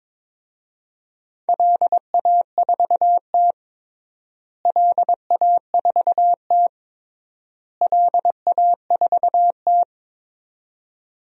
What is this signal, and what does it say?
Morse code, heard as a keyed tone, about 22 wpm, 700 Hz. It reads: LA4T LA4T LA4T
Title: DXCC-100 Call Signs - 4 Characters - Repeated 3x - 22wpm